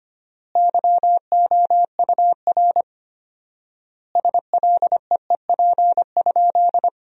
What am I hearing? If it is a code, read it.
Morse code, heard as a keyed tone, about 25 wpm, 700 Hz. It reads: YOUR SLEEP?
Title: 4x Word-Spacing - Sentences from Top 500 Words - 25wpm